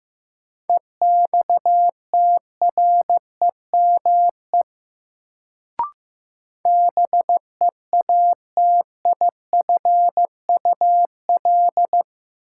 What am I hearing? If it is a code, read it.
Morse code, heard as a keyed tone, about 15 wpm, 700 Hz. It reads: EXTREME BEATIFUL
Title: Sentences from Top 1300 Words - New Word Drill - 15wpm